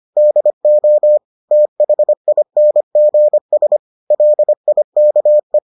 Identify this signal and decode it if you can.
Morse code, heard as a keyed tone, about 25 wpm, 600 Hz. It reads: DO THINGS LIKE